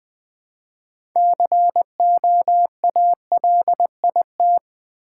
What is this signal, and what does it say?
Morse code, heard as a keyed tone, about 20 wpm, 700 Hz. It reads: COALIT
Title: Build A Word - 3 Letters At A Time - 9-Letter Words - 20wpm